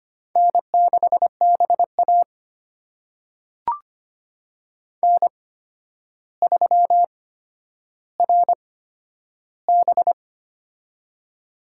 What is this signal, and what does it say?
Morse code, heard as a keyed tone, about 25 wpm, spaced as for 11 wpm, 700 Hz. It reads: N6BA N3RB
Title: USA Call Signs - 4 Characters - 8x Character Spacing - 25wpm